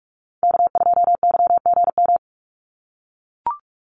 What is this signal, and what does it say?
Morse code, heard as a keyed tone, about 44 wpm, 700 Hz. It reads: K2YGM
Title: Call Signs - 45wpm